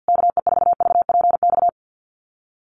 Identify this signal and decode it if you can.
Morse code, heard as a keyed tone, about 50 wpm, 700 Hz. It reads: KE4VPX
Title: USA Call Signs Repeated 2x 50wpm